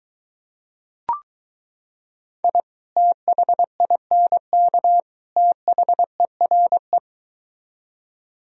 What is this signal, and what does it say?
Morse code, heard as a keyed tone, about 23 wpm, 700 Hz. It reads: I THINK THERE